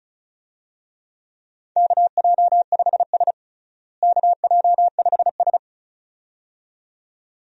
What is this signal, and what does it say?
Morse code, heard as a keyed tone, about 34 wpm, 700 Hz. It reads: KJ5S KJ5S
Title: USA Call Signs - 4 Characters - Repeated 2x - 35wpm